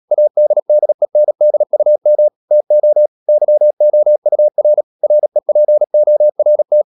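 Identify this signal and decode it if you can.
Morse code, heard as a keyed tone, about 37 wpm, 600 Hz. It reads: ADDENDUM TO YOUR REPORT
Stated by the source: Morse code practice at 35–39 wpm sending English text of Ken Follett's novel Never, from CW med Gnister.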